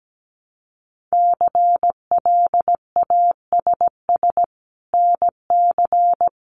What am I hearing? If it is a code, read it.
Morse code, heard as a keyed tone, about 17 wpm, 700 Hz. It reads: CLASS NC
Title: Winter Field Day Exchanges 17wpm